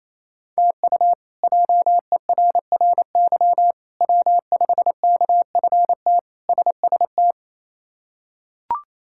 Morse code, heard as a keyed tone, about 27 wpm, 700 Hz. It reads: TU JERRY W5KFT SST
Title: K1USN SST QSO Practice - 28wpm